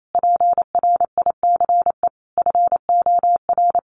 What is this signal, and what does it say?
Morse code, heard as a keyed tone, about 28 wpm, 700 Hz. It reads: PRICE FOR
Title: Sentences from Top 1100 Words 28wpm